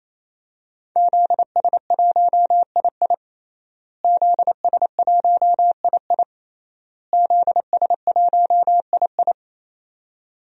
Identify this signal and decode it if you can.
Morse code, heard as a keyed tone, about 27 wpm, 700 Hz. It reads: ZS1II ZS1II ZS1II
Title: DXCC-100 Call Signs - 5 Characters - Repeated 3x - 28wpm